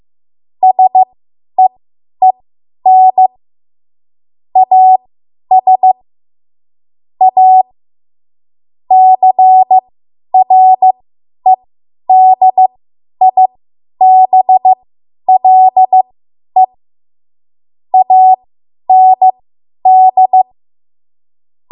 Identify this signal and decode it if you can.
Morse code, heard as a keyed tone, about 15 wpm, spaced as for 10 wpm, 750 Hz. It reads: SEEN AS A CREDIBLE AND
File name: Morse / 10WPM